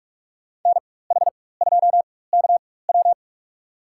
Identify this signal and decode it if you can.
Morse code, heard as a keyed tone, about 44 wpm, spaced as for 21 wpm, 690 Hz. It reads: NH2KW
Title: USA Call Signs - 4x Character Spacing - 45wpm